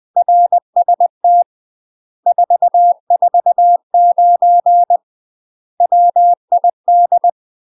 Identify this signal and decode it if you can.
Morse code, heard as a keyed tone, about 20 wpm, 700 Hz. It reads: RST 449 WID
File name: UK MASTER QSOs TRAINING / 20WPM / UK MASTER QSOs 0017